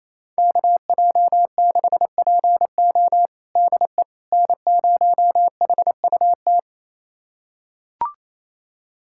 Morse code, heard as a keyed tone, about 28 wpm, 700 Hz. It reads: KJ6PO DE N0HUT
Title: USA Callsign DE Callsign - 5 Characters - 28wpm